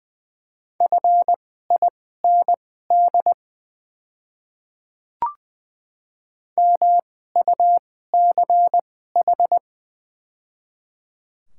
Morse code, heard as a keyed tone, about 20 wpm, spaced as for 14 wpm, 690 Hz. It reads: FIND MUCH
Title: 2x Character Spacing - Top 200 Words - 20wpm